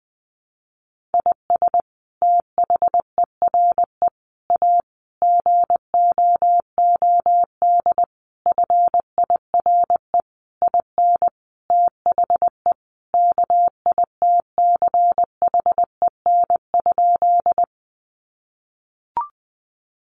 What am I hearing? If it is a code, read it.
Morse code, heard as a keyed tone, about 19 wpm, 700 Hz. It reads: IS THERE A GOOD FIRE IN THE KITCHEN?
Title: Sentences from Top 1400 Words 20wpm